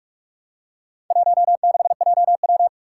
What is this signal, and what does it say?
Morse code, heard as a keyed tone, about 44 wpm, 700 Hz. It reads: 16JW
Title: Four Challenging Letter-Number Combinations 45wpm